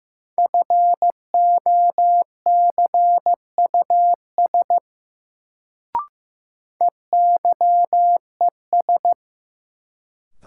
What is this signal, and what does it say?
Morse code, heard as a keyed tone, about 15 wpm, 700 Hz. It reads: FOCUS EYES
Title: Sentences from Top 1000 Words - Review Part 4 - New Word Drill - 15wpm